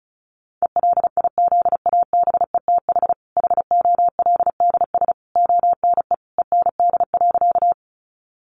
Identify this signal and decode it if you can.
Morse code, heard as a keyed tone, about 34 wpm, 700 Hz. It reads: ELIZABETH HOLDS ONE END.
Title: Easy Phrases 35wpm